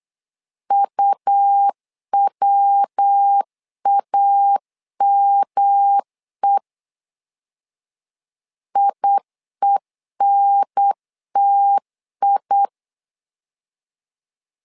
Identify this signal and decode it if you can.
Morse code, heard as a keyed tone, about 8 wpm, 800 Hz. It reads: UWAME IENTI
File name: LEZIONI / 3 MWRO / 3 MW 8wpm 2501